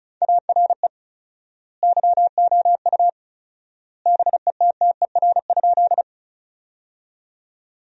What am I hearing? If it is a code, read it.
Morse code, heard as a keyed tone, about 34 wpm, 700 Hz. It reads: ARE YOU BETTER?
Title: 4x Word-Spacing - Sentences from Top 400 Words - 35wpm